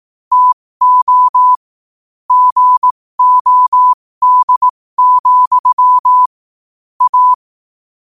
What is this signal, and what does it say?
Morse code, heard as a keyed tone, about 18 wpm, spaced as for 15 wpm, 1000 Hz. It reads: TO GOD, A
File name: qotdfiles / 15wpm